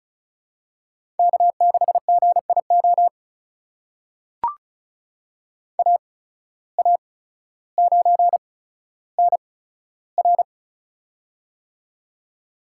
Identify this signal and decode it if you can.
Morse code, heard as a keyed tone, about 36 wpm, spaced as for 15 wpm, 700 Hz. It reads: K6GIO AA9NR
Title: USA Call Signs - 5 Characters - 8x Character Spacing - 35wpm